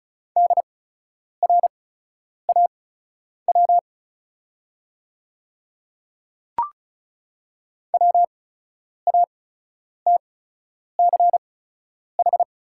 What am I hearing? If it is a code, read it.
Morse code, heard as a keyed tone, about 36 wpm, spaced as for 10 wpm, 700 Hz. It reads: DRAW WATCH
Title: 8x Character Spacing - Top 300 Words - 35wpm